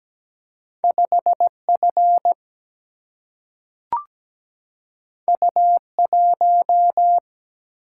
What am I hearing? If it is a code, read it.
Morse code, heard as a keyed tone, about 17 wpm, 700 Hz. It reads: 5 F U 1
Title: Single-Letter Single-Number 17wpm